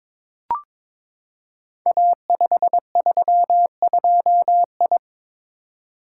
Morse code, heard as a keyed tone, about 22 wpm, 700 Hz. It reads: A532I
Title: Code Groups - 5 Letter:Numbers - 22wpm